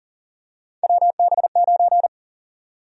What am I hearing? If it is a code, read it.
Morse code, heard as a keyed tone, about 40 wpm, 700 Hz. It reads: WB9
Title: USA Call Sign Prefixes - 40wpm